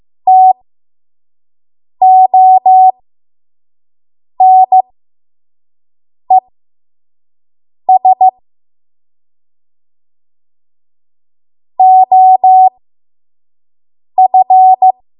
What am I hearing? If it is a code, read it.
Morse code, heard as a keyed tone, about 15 wpm, spaced as for 5 wpm, 750 Hz. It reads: TONES OF
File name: Morse / 05WPM